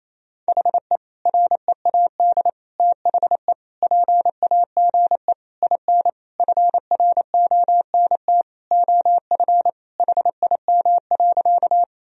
Morse code, heard as a keyed tone, about 28 wpm, 700 Hz. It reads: HE READ THE PAGE IN FRONT OF HIM.
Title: Sentences from Top 1000 Words - Review Part 1 - 28wpm